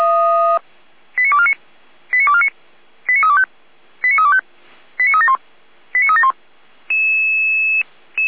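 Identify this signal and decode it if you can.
Five-tone selective call (ZVEI), heard as quick runs of five tones, tones of about 70 ms. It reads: 89249 89249 89325 89325 89371 89371
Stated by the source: ZVEI five-tone selective call received by SDR.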